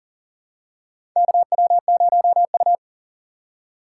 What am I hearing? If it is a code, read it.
Morse code, heard as a keyed tone, about 40 wpm, 700 Hz. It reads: KW0U
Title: USA Call Signs - 3x Character Spacing - 40wpm